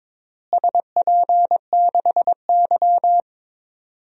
Morse code, heard as a keyed tone, about 21 wpm, 700 Hz. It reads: SP6Y
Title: DXCC-100 Call Signs - 4 Characters - Repeated 2x - 22wpm